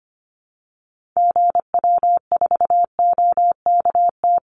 Morse code, heard as a keyed tone, about 25 wpm, 700 Hz. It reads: GW4OKT
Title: Call Signs - 25wpm